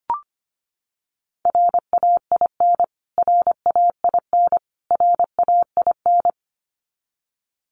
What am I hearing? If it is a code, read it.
Morse code, heard as a keyed tone, about 25 wpm, 700 Hz. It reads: RAIN RAIN RAIN
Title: Top 500 Words Repeated 25wpm